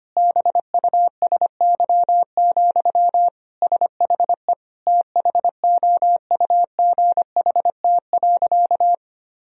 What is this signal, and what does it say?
Morse code, heard as a keyed tone, about 25 wpm, 700 Hz. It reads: BUSY, SHE THOUGHT.
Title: Sentences from Top 1000 Words - Review Part 3 - 25wpm